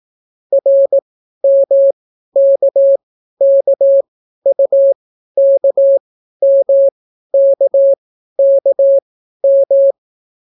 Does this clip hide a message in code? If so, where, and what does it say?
Yes: Morse code from 0.5 to 9.9 s, heard as a keyed tone, about 18 wpm, spaced as for 12 wpm, 550 Hz. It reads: RMKKUKMKKM